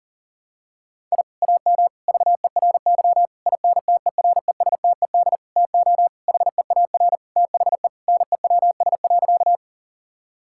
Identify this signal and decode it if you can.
Morse code, heard as a keyed tone, about 40 wpm, 700 Hz. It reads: I AM VERY INTERESTED TO HEAR THE NEWS.
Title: Sentences from Top 1000 Words - Review Part 6 - 40wpm